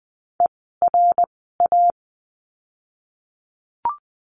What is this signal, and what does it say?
Morse code, heard as a keyed tone, about 20 wpm, 700 Hz. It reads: E R A
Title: 054 Lesson 10f - Letter r - Words 20wpm